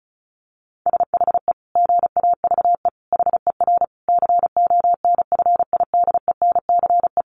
Morse code, heard as a keyed tone, about 34 wpm, 700 Hz. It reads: SHE GAVE HER CONFIDENCE
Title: Sentences from Top 1500 Words 35wpm